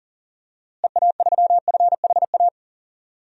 Morse code, heard as a keyed tone, about 40 wpm, 700 Hz. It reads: EA3FHA
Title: DXCC-100 Call Signs - 6 Characters - Repeated 3x - 40wpm